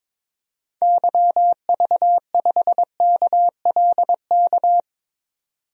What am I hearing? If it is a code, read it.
Morse code, heard as a keyed tone, about 21 wpm, 700 Hz. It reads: YV5KLK
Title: DXCC-100 Call Signs - 6 Characters - 22wpm